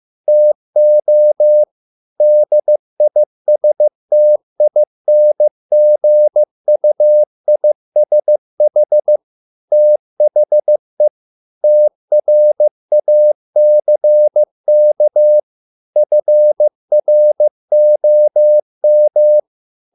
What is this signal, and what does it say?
Morse code, heard as a keyed tone, about 15 wpm, 600 Hz. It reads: TO DISTINGUISH THE TRACK FROM